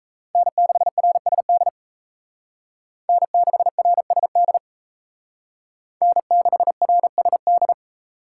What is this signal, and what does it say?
Morse code, heard as a keyed tone, about 36 wpm, 700 Hz. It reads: N6RSD N6RSD N6RSD